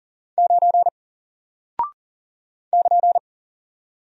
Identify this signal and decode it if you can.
Morse code, heard as a keyed tone, about 40 wpm, 700 Hz. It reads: 9 <KN>
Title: Single Character (Letter:Number:Punctuation:Prosign) Alt 40wpm